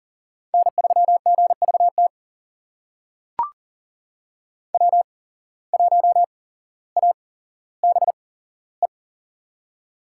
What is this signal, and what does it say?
Morse code, heard as a keyed tone, about 40 wpm, spaced as for 17 wpm, 700 Hz. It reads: N3GVT W1ABE
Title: USA Call Signs - 5 Characters - 8x Character Spacing - 40wpm